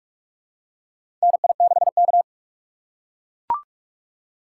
Morse code, heard as a keyed tone, about 44 wpm, 700 Hz. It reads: NI6K
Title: USA Call Signs - 4 Characters - 6x Character Spacing - 45wpm